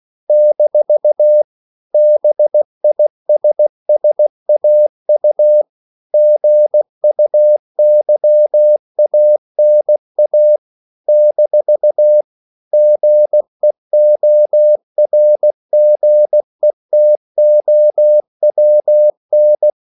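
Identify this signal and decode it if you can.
Morse code, heard as a keyed tone, about 16 wpm, 600 Hz. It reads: - BISSAU GUYANA - GEORGETOWN
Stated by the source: CW practice of national capitals at 16 words per minute, from ON6ZQ.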